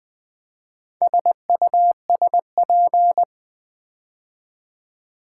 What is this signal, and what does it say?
Morse code, heard as a keyed tone, about 20 wpm, 700 Hz. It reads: SUSP
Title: Build A Word - 4 Letters At A Time - 10-Letter Words - 20wpm